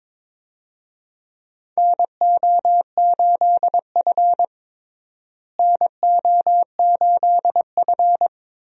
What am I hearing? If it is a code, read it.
Morse code, heard as a keyed tone, about 22 wpm, 700 Hz. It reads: NO8F NO8F
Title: USA Call Signs - 4 Characters - Repeated 3x - 22wpm